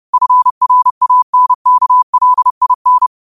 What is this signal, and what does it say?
Morse code, heard as a keyed tone, about 30 wpm, 1000 Hz. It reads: FRANKLIN